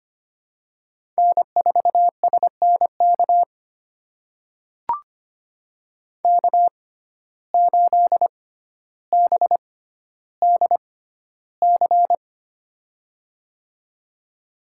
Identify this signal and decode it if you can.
Morse code, heard as a keyed tone, about 25 wpm, spaced as for 13 wpm, 700 Hz. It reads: N4SNK K8BDC
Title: USA Call Signs - 5 Characters - 6x Character Spacing - 25wpm